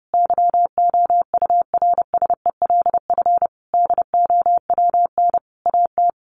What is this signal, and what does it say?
Morse code, heard as a keyed tone, about 30 wpm, 700 Hz. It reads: YOURSELF DOWN AT
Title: Sentences from Top 1200 Words 30wpm